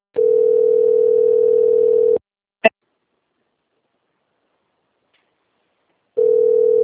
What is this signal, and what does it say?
Ringback tone (North America), 440 + 480 Hz.